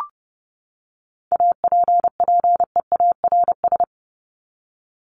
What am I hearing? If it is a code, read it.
Morse code, heard as a keyed tone, about 30 wpm, 700 Hz. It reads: APPEARS
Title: ES:S Suffix - Top 100 Words with Base Words 30wpm